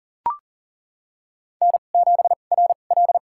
Morse code, heard as a keyed tone, about 37 wpm, spaced as for 28 wpm, 690 Hz. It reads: N7RL
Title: USA Call Signs - 4 Characters - 2x Character Spacing - 40wpm